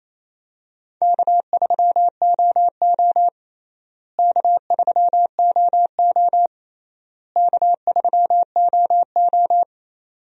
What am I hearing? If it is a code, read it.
Morse code, heard as a keyed tone, about 28 wpm, 700 Hz. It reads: K3OO K3OO K3OO
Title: USA Call Signs - 4 Characters - Repeated 3x - 28wpm